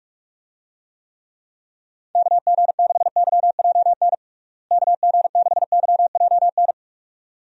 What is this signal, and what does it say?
Morse code, heard as a keyed tone, about 46 wpm, 700 Hz. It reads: KG6YJN KG6YJN
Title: USA Call Signs - 6 Characters - Repeated 2x - 45wpm